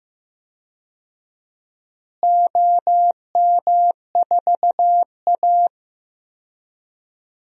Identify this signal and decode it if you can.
Morse code, heard as a keyed tone, about 15 wpm, 700 Hz. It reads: OM4A
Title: DXCC-100 Call Signs - 4 Characters - Repeated 2x - 15wpm